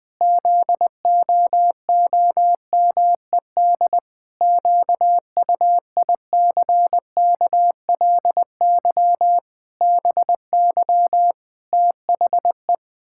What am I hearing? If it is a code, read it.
Morse code, heard as a keyed tone, about 21 wpm, 700 Hz. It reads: ZOOMED QUICKLY BY THE
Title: Pangram Challenge v2 - 20wpm